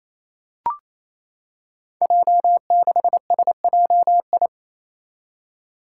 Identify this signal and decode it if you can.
Morse code, heard as a keyed tone, about 28 wpm, 700 Hz. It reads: J6SJI
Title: Five Challenging Letter-Number Combinations 28wpm